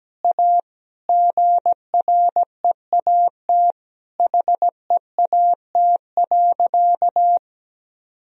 Morse code, heard as a keyed tone, about 17 wpm, 700 Hz. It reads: A GREAT HEAT.
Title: Sentences from Top 1000 Words - Review Part 3 - 17wpm